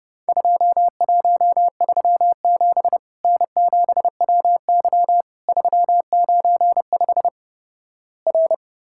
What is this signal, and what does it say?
Morse code, heard as a keyed tone, about 32 wpm, 700 Hz. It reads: 2137 N7WY 395 R
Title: Worked All Europe DX Contest - QTC Exchange - 30wpm